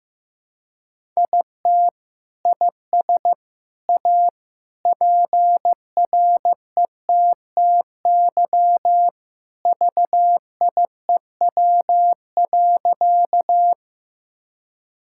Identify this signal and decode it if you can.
Morse code, heard as a keyed tone, about 16 wpm, 700 Hz. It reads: IT IS A PRETTY VIEW.
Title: Sentences from Top 1000 Words - Review Part 6 - 15wpm